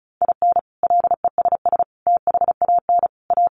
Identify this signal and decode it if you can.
Morse code, heard as a keyed tone, about 34 wpm, 700 Hz. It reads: IN LESS THAN A
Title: Sentences from Top 1300 Words 35wpm